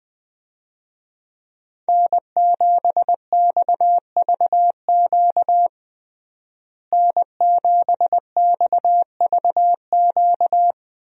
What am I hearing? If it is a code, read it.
Morse code, heard as a keyed tone, about 20 wpm, 700 Hz. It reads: N7XVQ N7XVQ
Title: USA Call Signs - 5 Characters - Repeated 2x - 20wpm